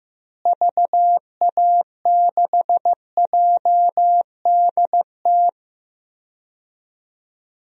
Morse code, heard as a keyed tone, about 15 wpm, 700 Hz. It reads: VA6JDT
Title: DXCC-100 Call Signs - 6 Characters - Repeated 3x - 15wpm